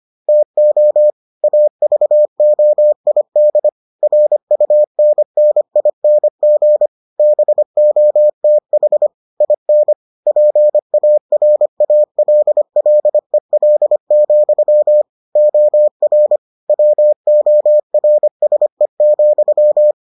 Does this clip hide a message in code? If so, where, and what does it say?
Yes: Morse code from 0.3 to 19.9 s, heard as a keyed tone, about 25 wpm, 600 Hz. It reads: TO AVOID RUNNING BOTH IN PARALLEL, OR WORSE,